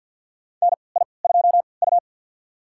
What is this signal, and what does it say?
Morse code, heard as a keyed tone, about 48 wpm, spaced as for 28 wpm, 700 Hz. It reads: NI2U
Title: USA Call Signs - 4 Characters - 3x Character Spacing - 50wpm